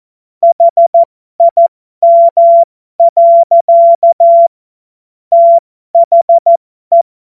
Morse code, heard as a keyed tone, about 14 wpm, spaced as for 12 wpm, 680 Hz. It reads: HIM. THE